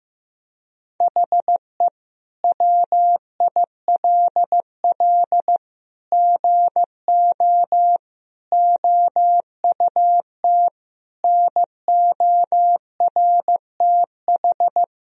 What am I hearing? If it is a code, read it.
Morse code, heard as a keyed tone, about 15 wpm, 700 Hz. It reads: HE WILL GO OUT NORTH
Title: Sentences from Top 1000 Words - Review Part 1 - 15wpm